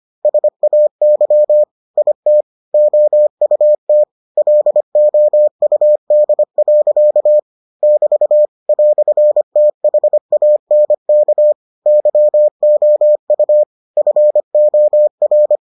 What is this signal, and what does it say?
Morse code, heard as a keyed tone, about 25 wpm, 600 Hz. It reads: SAY IT OUT LOUD. <BT> "THANK YOU FOR